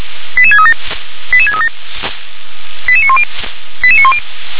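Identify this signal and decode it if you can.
Five-tone selective call (ZVEI), heard as quick runs of five tones, tones of about 70 ms. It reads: 88637 88637 88910 88910
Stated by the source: ZVEI five-tone call received on an SDR.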